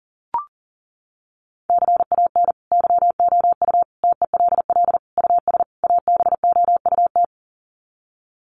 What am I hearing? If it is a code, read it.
Morse code, heard as a keyed tone, about 40 wpm, 700 Hz. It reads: CAN YOU TELL US ABOUT
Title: Sets of 5 Words Encore2 40wpm